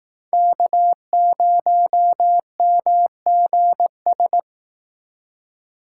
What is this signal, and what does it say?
Morse code, heard as a keyed tone, about 18 wpm, 700 Hz. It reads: K0MGS
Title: Speed Racing - USA 1x3 Call Signs 15wpm